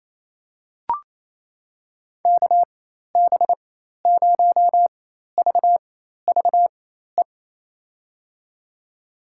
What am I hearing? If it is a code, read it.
Morse code, heard as a keyed tone, about 27 wpm, spaced as for 13 wpm, 700 Hz. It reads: KB0VVE
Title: USA Call Signs - 6 Characters - 4x Character Spacing - 28wpm